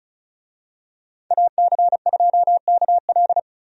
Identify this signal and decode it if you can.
Morse code, heard as a keyed tone, about 34 wpm, 700 Hz. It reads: AC2KL
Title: USA Call Signs - 5 Characters - 35wpm